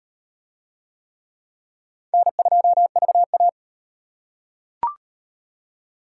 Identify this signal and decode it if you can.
Morse code, heard as a keyed tone, about 38 wpm, 700 Hz. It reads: N2VA